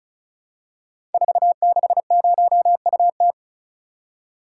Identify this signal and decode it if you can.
Morse code, heard as a keyed tone, about 36 wpm, 690 Hz. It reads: 460UT
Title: Code Groups - 5 Letter:Numbers - 35wpm